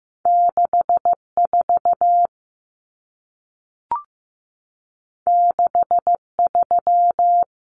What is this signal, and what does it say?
Morse code, heard as a keyed tone, about 15 wpm, 700 Hz. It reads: 64 63
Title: Numbers - 2-Digit 15wpm